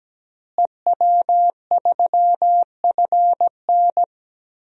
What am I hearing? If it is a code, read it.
Morse code, heard as a keyed tone, about 17 wpm, 700 Hz. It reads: EW3FN
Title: DXCC-100 Call Signs - 5 Characters - Repeated 2x - 17wpm